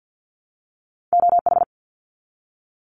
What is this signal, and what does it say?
Morse code, heard as a keyed tone, about 50 wpm, 700 Hz. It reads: OH
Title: Top 500 Words 50wpm